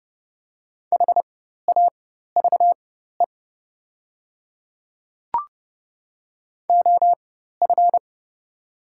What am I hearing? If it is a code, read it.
Morse code, heard as a keyed tone, about 30 wpm, spaced as for 14 wpm, 700 Hz. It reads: HAVE OF
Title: 4x Character Spacing - Top 300 Words - 30wpm